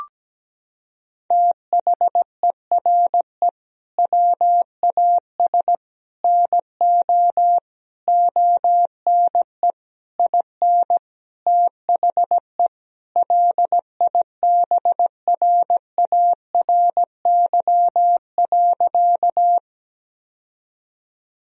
Morse code, heard as a keyed tone, about 17 wpm, 700 Hz. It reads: THERE WAS NO ONE IN THE LIBRARY.
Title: Sentences from Top 1000 Words - Review Part 4 - 17wpm